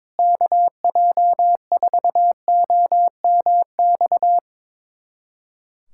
Morse code, heard as a keyed tone, about 22 wpm, 700 Hz. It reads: KJ4OMX
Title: USA Callsign DE Callsign - 6 Characters - 22wpm